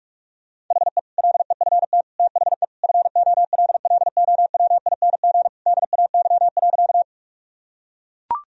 Morse code, heard as a keyed tone, about 44 wpm, 700 Hz. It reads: HE LEFT THE FOLLOWING DAY.
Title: Sentences from Top 1000 Words - Review Part 4 - 45wpm